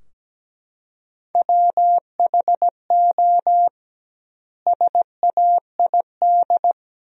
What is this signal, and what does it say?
Morse code, heard as a keyed tone, about 17 wpm, 700 Hz. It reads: WHO SAID
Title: 2x Word-Spacing - Sentences from Top 300 Words - 17wpm